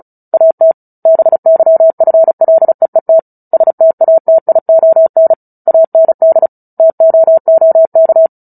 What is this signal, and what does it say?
Morse code, heard as a keyed tone, about 34 wpm, 650 Hz. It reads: AT BYFLEET STATION AND TOOK
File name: Book Two / 09-Chapter 9 - 35 WPM